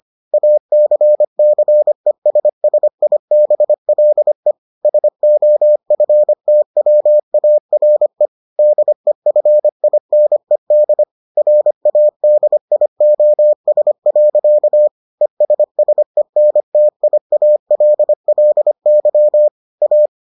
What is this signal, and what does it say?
Morse code, heard as a keyed tone, about 25 wpm, 600 Hz. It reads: ACCESSIBLE SOFTWARE DEFINED RADIOS. ESSENTIALLY A